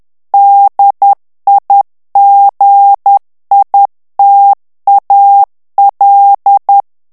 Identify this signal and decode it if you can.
Morse code, heard as a keyed tone, about 10 wpm, 800 Hz. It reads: DIGITAL
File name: ANGLAIS texte clair 3 vitesse 10 mots minute